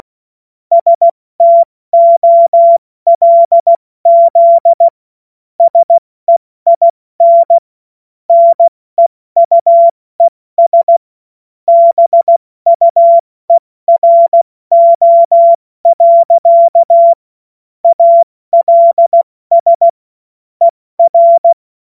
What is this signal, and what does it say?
Morse code, heard as a keyed tone, about 16 wpm, 680 Hz. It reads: STOLZ SEIN NEUES BUERO. ALS ER
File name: Morsekurs / Witze / Witz 06. Dezember 2022 70 F80